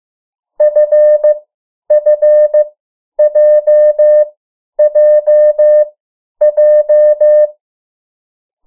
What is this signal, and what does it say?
Morse code, heard as a keyed tone, about 15 wpm, spaced as for 10 wpm, 600 Hz. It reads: FFJJJ
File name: morse mp3 / 03 urok 2